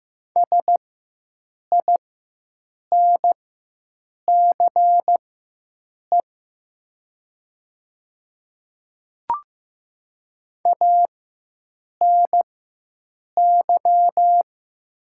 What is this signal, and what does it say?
Morse code, heard as a keyed tone, about 15 wpm, spaced as for 7 wpm, 700 Hz. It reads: SINCE ANY
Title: 4x Character Spacing - Top 300 Words - 15wpm